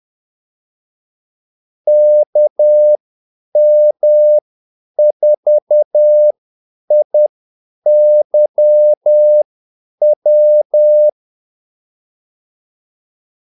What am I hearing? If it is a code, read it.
Morse code, heard as a keyed tone, about 10 wpm, spaced as for 8 wpm, 600 Hz. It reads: KM4IYW
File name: Callsigns-10-8wpm